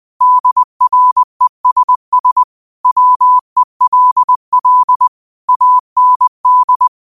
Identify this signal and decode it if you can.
Morse code, heard as a keyed tone, about 20 wpm, 1000 Hz. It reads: DRESS WELL AND
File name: qotdfiles / 20wpm